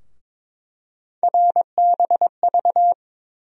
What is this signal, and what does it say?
Morse code, heard as a keyed tone, about 22 wpm, 700 Hz. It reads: RBV
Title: USA Call Sign Postfixes - 3 Letters 22wpm